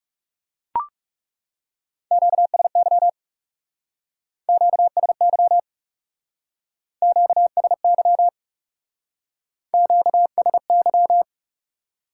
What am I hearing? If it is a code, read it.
Morse code, heard as a keyed tone, about 38 wpm, 700 Hz. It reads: QSY QSY QSY QSY